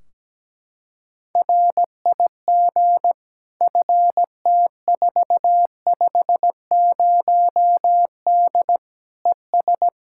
Morse code, heard as a keyed tone, about 17 wpm, 700 Hz. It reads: RIG FT450D ES
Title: Practice QSO - Exchange 2 - Info - 17wpm